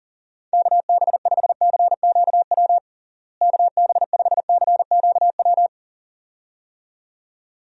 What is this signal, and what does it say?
Morse code, heard as a keyed tone, about 39 wpm, 700 Hz. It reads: KB5CQW KB5CQW
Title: USA Call Signs - 6 Characters - Repeated 2x - 40wpm